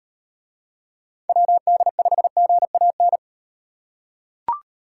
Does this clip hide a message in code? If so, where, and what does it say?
Yes: Morse code from 1.3 to 3.2 s, heard as a keyed tone, about 38 wpm, 700 Hz. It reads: WD5GAN